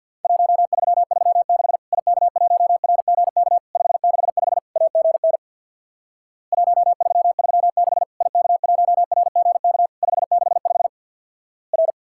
Morse code, heard as a keyed tone, about 50 wpm, 700 Hz. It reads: 1336 IK1RGK 565 AGN 1336 IK1RGK 565 R
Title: Worked All Europe DX Contest - QTC Exchange with Repeats - 50wpm